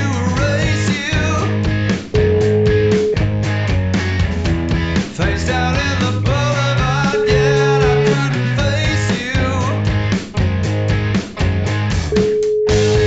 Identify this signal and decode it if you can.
Ringback tone (Europe, 425 Hz), with noise.